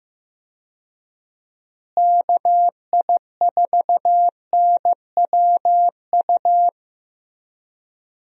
Morse code, heard as a keyed tone, about 15 wpm, 700 Hz. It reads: KI4NWU
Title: USA Call Signs - 6 Characters - Repeated 3x - 15wpm